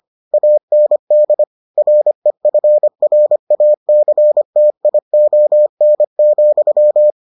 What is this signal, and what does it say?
Morse code, heard as a keyed tone, about 25 wpm, 600 Hz. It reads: AND REFRACTION,